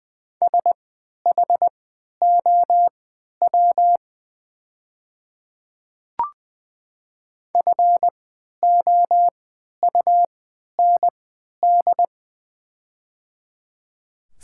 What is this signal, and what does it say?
Morse code, heard as a keyed tone, about 21 wpm, spaced as for 12 wpm, 700 Hz. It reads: SHOW FOUND
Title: 3x Character Spacing - Top 300 Words - 20wpm